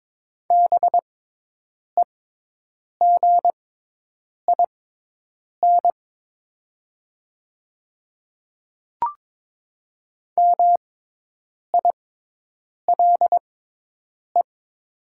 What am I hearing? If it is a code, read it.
Morse code, heard as a keyed tone, about 22 wpm, spaced as for 8 wpm, 700 Hz. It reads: BEGIN MILE
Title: 6x Character Spacing - Top 300 Words - 22wpm